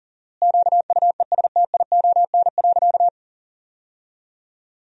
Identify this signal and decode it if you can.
Morse code, heard as a keyed tone, about 40 wpm, 700 Hz. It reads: QUESTION.
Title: 4x Word-Spacing - Sentences from Top 400 Words - 40wpm